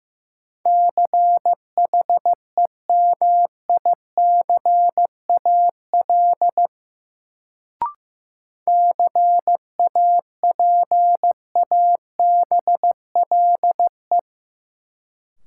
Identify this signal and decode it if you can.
Morse code, heard as a keyed tone, about 15 wpm, 700 Hz. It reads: CHEMICAL CAPABLE
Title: Sentences from Top 1200 Words - New Word Drill Alt - 15wpm